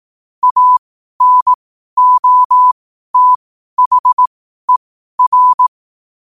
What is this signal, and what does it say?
Morse code, heard as a keyed tone, about 18 wpm, spaced as for 12 wpm, 1000 Hz. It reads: ANOTHER
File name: qotdfiles / 12wpm